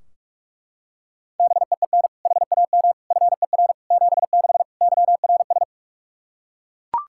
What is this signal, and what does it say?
Morse code, heard as a keyed tone, about 41 wpm, 700 Hz. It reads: BEEN HAM FER 76 YRS
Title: Practice QSO - Exchange 3 - More Info - 45wpm